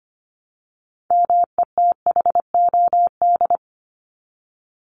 Morse code, heard as a keyed tone, about 25 wpm, 700 Hz. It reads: METHOD
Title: Top 501-750 Words 25wpm